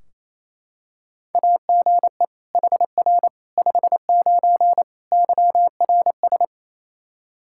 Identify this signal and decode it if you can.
Morse code, heard as a keyed tone, about 28 wpm, 700 Hz. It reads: AGE HR 59 YRS
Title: Practice QSO - Exchange 3 - More Info - 28wpm